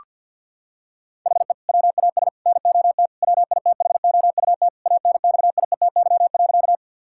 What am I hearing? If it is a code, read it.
Morse code, heard as a keyed tone, about 46 wpm, 700 Hz. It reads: HE WAS NOT WITHOUT ANXIETY.
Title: Sentences from Top 1600 Words 50wpm